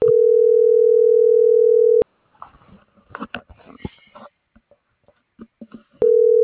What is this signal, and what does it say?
Ringback tone (North America), 440 + 480 Hz.